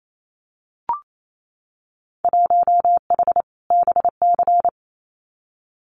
Morse code, heard as a keyed tone, about 28 wpm, 700 Hz. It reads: 1H BC
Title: Winter Field Day Exchanges 28wpm